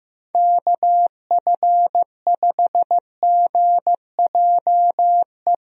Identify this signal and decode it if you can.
Morse code, heard as a keyed tone, about 15 wpm, 700 Hz. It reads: KF5GJE
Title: USA Call Signs - 6 Characters - Repeated 3x - 15wpm